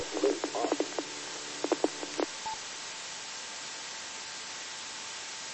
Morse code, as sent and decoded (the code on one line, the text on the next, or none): none